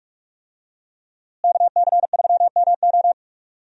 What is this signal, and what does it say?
Morse code, heard as a keyed tone, about 46 wpm, 700 Hz. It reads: KC3MO
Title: USA Call Signs - 5 Characters - Repeated 3x - 45wpm